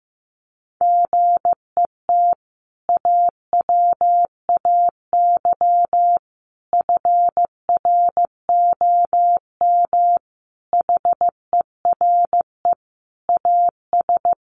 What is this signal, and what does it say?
Morse code, heard as a keyed tone, about 15 wpm, 700 Hz. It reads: GET AWAY FROM HERE AS
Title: Sentences from Top 600 Words 15wpm